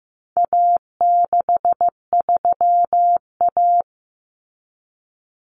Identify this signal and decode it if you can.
Morse code, heard as a keyed tone, about 15 wpm, 700 Hz. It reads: A63A
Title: Common Cut Numbers - 4-Digit 15wpm